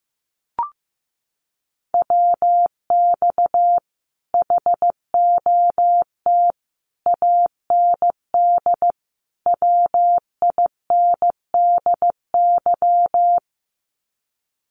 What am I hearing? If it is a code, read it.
Morse code, heard as a keyed tone, about 15 wpm, 700 Hz. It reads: WX HOT AND WINDY
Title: Common QSO Phrases Encore 15wpm